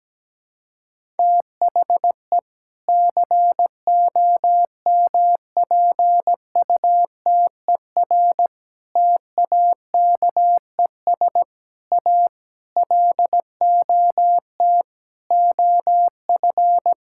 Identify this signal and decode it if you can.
Morse code, heard as a keyed tone, about 17 wpm, 700 Hz. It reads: THE COMPUTER TAKES A LOT OF
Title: Sentences from Top 1000 Words - Review Part 6 - 17wpm